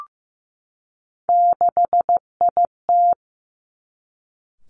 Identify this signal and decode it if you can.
Morse code, heard as a keyed tone, about 15 wpm, 700 Hz. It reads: 6IT
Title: Three Challenging Letter-Number Combinations 15wpm